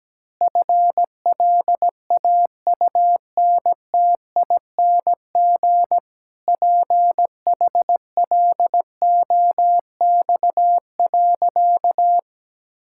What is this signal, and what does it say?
Morse code, heard as a keyed tone, about 17 wpm, 700 Hz. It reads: FLAUNTING PHLOX.
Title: Pangram Challenge v2 - 17wpm